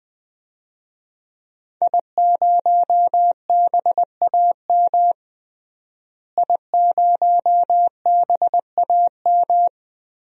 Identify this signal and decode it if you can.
Morse code, heard as a keyed tone, about 20 wpm, 700 Hz. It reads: I0BAM I0BAM
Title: DXCC-100 Call Signs - 5 Characters - Repeated 3x - 20wpm